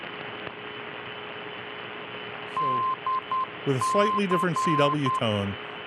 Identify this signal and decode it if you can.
Morse code, heard as a keyed tone, about 10 wpm, 1040 Hz, with speech and radio noise. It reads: DC